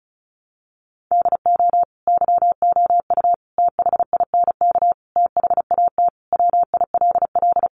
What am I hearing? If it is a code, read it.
Morse code, heard as a keyed tone, about 34 wpm, 700 Hz. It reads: DO YOU THINK THAT WILL
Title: Sentences from Top 800 Words 35wpm